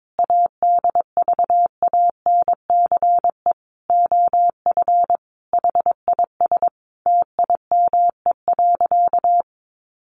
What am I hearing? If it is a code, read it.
Morse code, heard as a keyed tone, about 22 wpm, 700 Hz. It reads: ADVANCE OF HIS TIME.
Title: Sentences from Top 1400 Words 22wpm